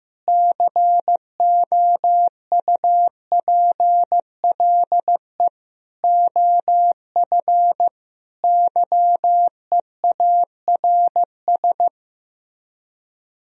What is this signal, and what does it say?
Morse code, heard as a keyed tone, about 15 wpm, 700 Hz. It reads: COUPLE OF YEARS
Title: Speed-Racing - Sets of 5 Words 15wpm